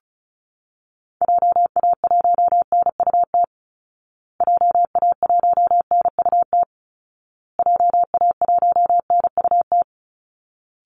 Morse code, heard as a keyed tone, about 34 wpm, 700 Hz. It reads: JA1NUT JA1NUT JA1NUT
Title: Call Signs Repeated 2x - 35wpm